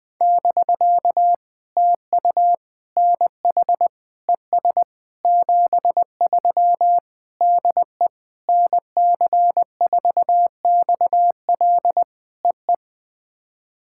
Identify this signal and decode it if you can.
Morse code, heard as a keyed tone, about 20 wpm, 700 Hz. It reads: <BK> TU NH ES 73 DE NC4XL EE
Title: POTA QSO Practice - 20wpm